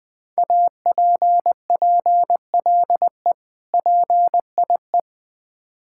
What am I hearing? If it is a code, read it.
Morse code, heard as a keyed tone, about 20 wpm, 700 Hz. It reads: APPLE PIE
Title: Difficult Words with Hints - 20wpm